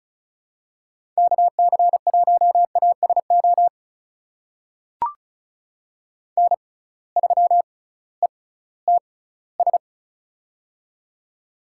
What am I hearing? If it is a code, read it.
Morse code, heard as a keyed tone, about 36 wpm, spaced as for 19 wpm, 700 Hz. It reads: KC1ASO N3ETS
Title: USA Call Signs - 6x Character Spacing - 35wpm